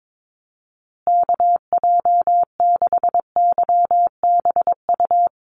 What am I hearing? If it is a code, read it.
Morse code, heard as a keyed tone, about 22 wpm, 700 Hz. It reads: KJ6YBU
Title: USA Call Signs Repeated 2x 22wpm